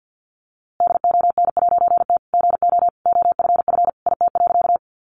Morse code, heard as a keyed tone, about 49 wpm, 700 Hz. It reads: DON'T GO OFF IT.
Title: A Sound Of Thunder 50wpm